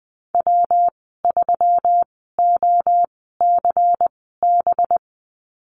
Morse code, heard as a keyed tone, about 19 wpm, spaced as for 14 wpm, 700 Hz. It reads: W3OCB
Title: 193 Lesson 31g - Letter v - Call Signs 20wpm